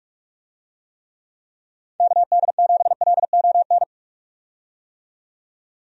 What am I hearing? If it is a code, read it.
Morse code, heard as a keyed tone, about 46 wpm, 700 Hz. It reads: KD7LON
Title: USA Call Signs - 6 Characters - 45wpm